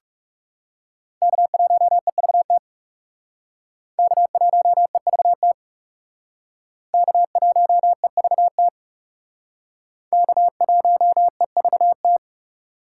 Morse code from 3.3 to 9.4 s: K1EVT K1EVT

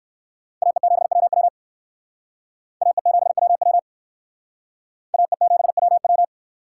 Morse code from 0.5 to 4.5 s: AE7WW AE7WW

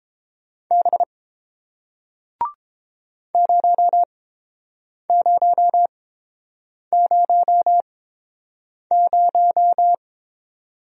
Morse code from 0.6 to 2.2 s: B

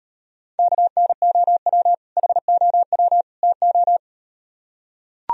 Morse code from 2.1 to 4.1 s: HOW TO

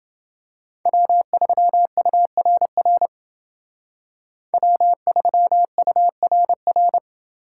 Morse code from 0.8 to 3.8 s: W3URR

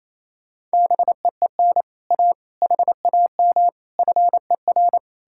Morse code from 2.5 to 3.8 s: HAM

